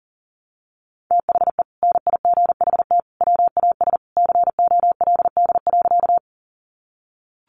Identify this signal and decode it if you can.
Morse code, heard as a keyed tone, about 40 wpm, 700 Hz. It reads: THE NIGHT WAS COLD.
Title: Sentences from Top 400 Words 40wpm